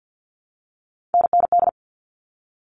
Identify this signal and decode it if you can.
Morse code, heard as a keyed tone, about 48 wpm, 700 Hz. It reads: NND
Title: All Cut Numbers - 3-Digit 50wpm